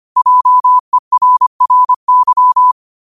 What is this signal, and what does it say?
Morse code, heard as a keyed tone, about 25 wpm, 1000 Hz. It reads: JERRY
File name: qotdfiles / 25wpm